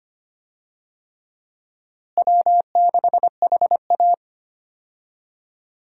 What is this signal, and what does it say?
Morse code, heard as a keyed tone, about 25 wpm, 700 Hz. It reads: W6HA
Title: USA Call Signs - 4 Characters - Repeated 3x - 25wpm